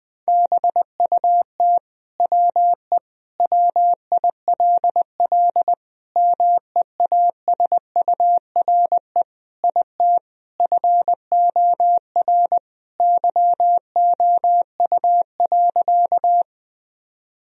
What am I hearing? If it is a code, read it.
Morse code, heard as a keyed tone, about 20 wpm, 700 Hz. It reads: BUT WE WILL MEASURE IT FOR YOU.
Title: Sentences from Top 1000 Words - Review Part 2 - 20wpm